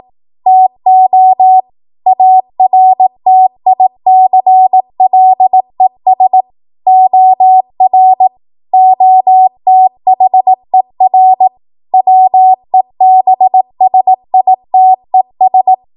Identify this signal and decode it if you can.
Morse code, heard as a keyed tone, about 18 wpm, 750 Hz. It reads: TO ARTICLES OR OTHER WEBSITES
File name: Morse / 18WPM